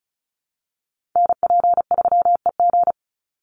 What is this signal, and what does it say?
Morse code, heard as a keyed tone, about 34 wpm, 700 Hz. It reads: NP3EG
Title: USA Call Signs 35wpm